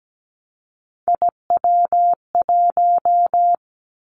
Morse code, heard as a keyed tone, about 17 wpm, 700 Hz. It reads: IW1
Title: Callsign Prefixes for Top 25 DXCC - 17wpm